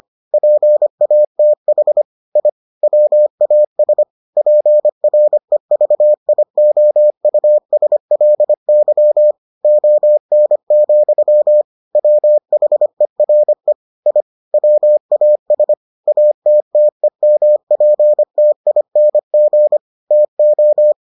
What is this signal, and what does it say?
Morse code, heard as a keyed tone, about 25 wpm, 600 Hz. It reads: PATH I WAS PREVIOUSLY ON, WHERE I WAS ATTEMPTING TO